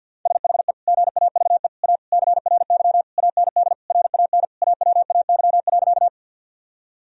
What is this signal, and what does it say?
Morse code, heard as a keyed tone, about 50 wpm, 700 Hz. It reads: SHE GAVE A CRY AND RAN AWAY.
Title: Sentences from Top 1000 Words - Review Part 3 - 50wpm